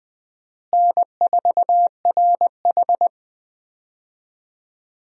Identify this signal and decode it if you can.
Morse code, heard as a keyed tone, about 20 wpm, 700 Hz. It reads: N4RH
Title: USA Call Signs - 4 Characters - Repeated 2x - 20wpm